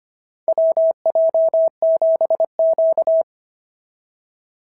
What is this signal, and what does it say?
Morse code, heard as a keyed tone, about 25 wpm, 650 Hz. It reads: WJ7Q
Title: SOTA QSO Practice 25wpm